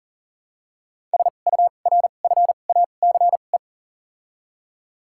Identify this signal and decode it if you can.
Morse code, heard as a keyed tone, about 40 wpm, 700 Hz. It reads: S U R F A C E
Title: 2x Character Spacing - Top 500 Words - 40wpm